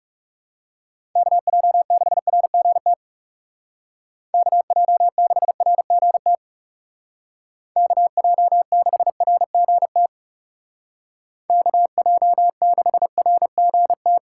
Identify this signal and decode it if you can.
Morse code, heard as a keyed tone, about 38 wpm, 700 Hz. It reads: KJ6RGT KJ6RGT KJ6RGT KJ6RGT
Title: Speed Racing - USA 2x3 Call Signs 30wpm